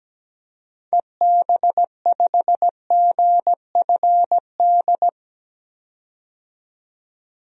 Morse code, heard as a keyed tone, about 17 wpm, 700 Hz. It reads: EB5GFD
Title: DXCC-100 Call Signs - 6 Characters - 17wpm